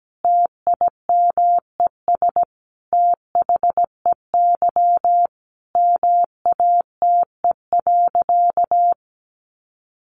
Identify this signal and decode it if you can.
Morse code, heard as a keyed tone, about 17 wpm, 700 Hz. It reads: TIMES THEY MATE.
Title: A Sound Of Thunder 17wpm